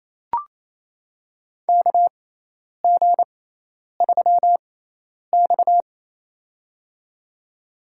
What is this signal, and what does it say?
Morse code, heard as a keyed tone, about 28 wpm, spaced as for 10 wpm, 700 Hz. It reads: KG3X
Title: USA Call Signs - 4 Characters - 6x Character Spacing - 28wpm